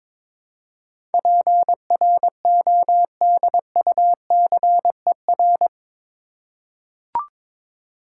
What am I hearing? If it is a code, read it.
Morse code, heard as a keyed tone, about 22 wpm, 700 Hz. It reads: PRODUCER
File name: practice / 22wpm / Eight Letter Words - 22wpm